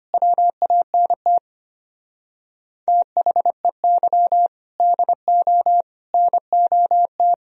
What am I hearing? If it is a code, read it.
Morse code, heard as a keyed tone, about 31 wpm, 700 Hz. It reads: WANT THEY DO NOT
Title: Speed-Racing - Sets of 4 Words 25wpm